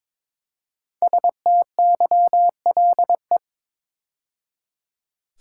Morse code, heard as a keyed tone, about 22 wpm, 700 Hz. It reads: STYLE
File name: 5 Letter Words - First 3 Letters Given - 22wpm